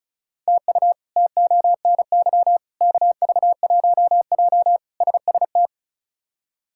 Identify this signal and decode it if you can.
Morse code, heard as a keyed tone, about 36 wpm, 700 Hz. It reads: TU TONY KV1J SST
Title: K1USN SST QSO Practice - 35wpm